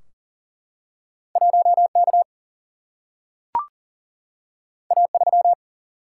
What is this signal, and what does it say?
Morse code, heard as a keyed tone, about 40 wpm, 700 Hz. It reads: 1 K A 3
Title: Single-Letter Single-Number 40wpm